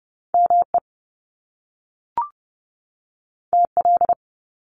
Morse code, heard as a keyed tone, about 30 wpm, 700 Hz. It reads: ME TL
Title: Top Two Letter Combinations in Words 30wpm